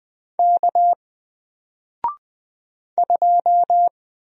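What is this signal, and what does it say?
Morse code, heard as a keyed tone, about 20 wpm, 700 Hz. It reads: K 2
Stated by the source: Single letter and number Morse drill at 20 wpm, from N4LCD.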